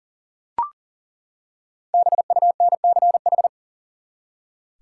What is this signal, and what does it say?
Morse code, heard as a keyed tone, about 40 wpm, 700 Hz. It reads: BUNCH
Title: Five Letter Words - 40wpm